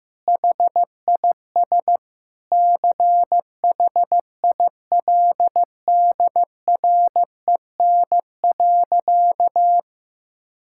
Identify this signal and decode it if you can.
Morse code, heard as a keyed tone, about 15 wpm, 700 Hz. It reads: HIS CHILDREN.
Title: Sentences from Top 1000 Words - Review Part 5 - 15wpm